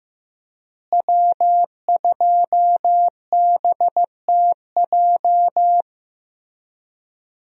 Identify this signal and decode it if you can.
Morse code, heard as a keyed tone, about 15 wpm, 700 Hz. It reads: W2BTJ
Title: USA Call Signs - 5 Characters - Repeated 2x - 15wpm